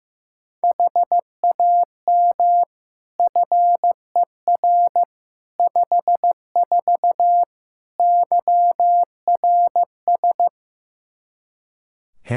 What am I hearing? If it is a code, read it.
Morse code, heard as a keyed tone, about 15 wpm, 700 Hz. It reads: HAM FER 54 YRS
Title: K1USN SST QSO Practice 15wpm